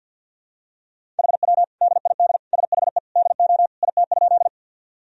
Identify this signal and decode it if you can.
Morse code, heard as a keyed tone, about 52 wpm, 700 Hz. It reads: HW DID SHE DO IT?
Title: Sentences from Top 100 Words and Abbreviations 50wpm